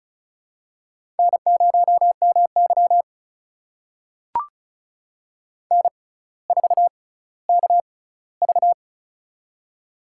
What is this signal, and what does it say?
Morse code, heard as a keyed tone, about 34 wpm, spaced as for 18 wpm, 700 Hz. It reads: N0MY N4KV
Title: USA Call Signs - 4 Characters - 6x Character Spacing - 35wpm